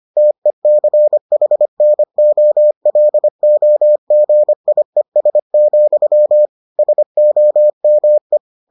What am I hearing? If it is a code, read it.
Morse code, heard as a keyed tone, about 25 wpm, 600 Hz. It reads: TECHNOLOGIES, SOME